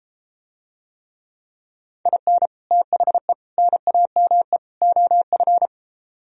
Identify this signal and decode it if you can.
Morse code, heard as a keyed tone, about 33 wpm, 700 Hz. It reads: IN THE NAME OF